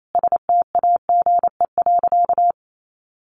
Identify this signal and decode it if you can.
Morse code, heard as a keyed tone, about 28 wpm, 700 Hz. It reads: STAGE.
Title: Sentences from Top 1500 Words 28wpm